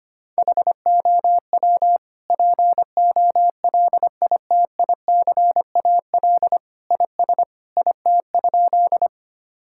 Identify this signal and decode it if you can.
Morse code, heard as a keyed tone, about 25 wpm, 700 Hz. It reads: HOW POLITICAL IS IT?
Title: Sentences from Top 1000 Words - Review Part 5 - 25wpm